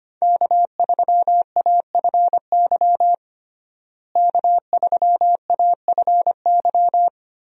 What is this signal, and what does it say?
Morse code, heard as a keyed tone, about 25 wpm, 700 Hz. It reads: K3AFY K3AFY
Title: USA Call Signs - 5 Characters - Repeated 2x - 25wpm